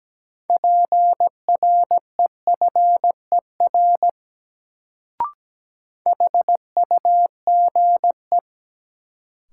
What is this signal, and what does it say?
Morse code, heard as a keyed tone, about 17 wpm, 700 Hz. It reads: PREFER HUGE
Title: Sentences from Top 1100 Words - New Word Drill Alt - 17wpm